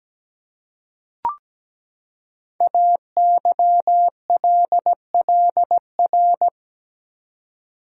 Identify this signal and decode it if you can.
Morse code, heard as a keyed tone, about 17 wpm, 700 Hz. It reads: AYLLR
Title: Code Groups - 5 Letters - 17wpm